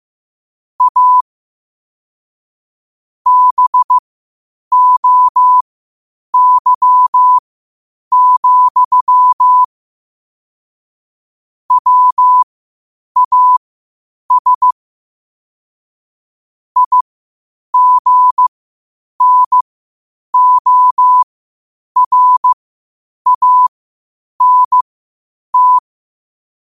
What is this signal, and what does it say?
Morse code, heard as a keyed tone, about 15 wpm, spaced as for 8 wpm, 1000 Hz. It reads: A BOY, WAS IGNORANT
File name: qotdfiles / 8wpm